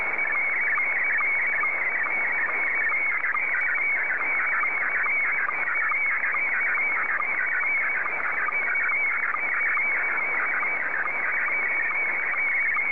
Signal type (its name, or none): SSTV (slow-scan television)